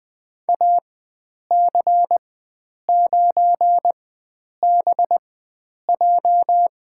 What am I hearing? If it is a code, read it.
Morse code, heard as a keyed tone, about 20 wpm, spaced as for 9 wpm, 700 Hz. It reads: AC9BJ
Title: USA Call Signs - 5 Characters - 4x Character Spacing - 20wpm